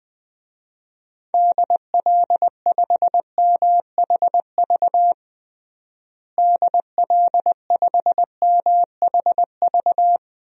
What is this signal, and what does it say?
Morse code, heard as a keyed tone, about 20 wpm, 700 Hz. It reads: DL5MHV DL5MHV
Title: DXCC-100 Call Signs - 6 Characters - Repeated 2x - 20wpm